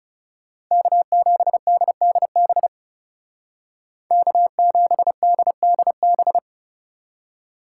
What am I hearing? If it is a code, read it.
Morse code, heard as a keyed tone, about 36 wpm, 700 Hz. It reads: K7DDB K7DDB
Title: Speed Racing - USA 1x3 Call Signs 30wpm